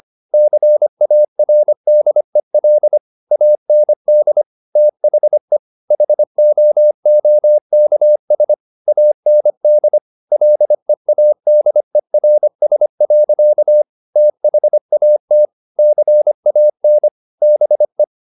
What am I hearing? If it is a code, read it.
Morse code, heard as a keyed tone, about 25 wpm, 600 Hz. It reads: CARDEL AND THE HOOKS AND LEADERS. THAT CAN BE
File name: the old man and the sea part 1 25wpm